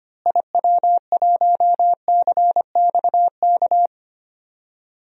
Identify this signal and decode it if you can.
Morse code, heard as a keyed tone, about 25 wpm, 700 Hz. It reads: IW1CXK
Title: DXCC-100 Call Signs - 6 Characters - Repeated 2x - 25wpm